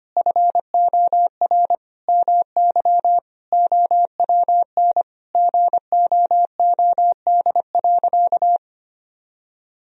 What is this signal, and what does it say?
Morse code, heard as a keyed tone, about 25 wpm, 700 Hz. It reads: FOR MY OWN GOOD.
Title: Sentences from Top 1000 Words - Review Part 4 - 25wpm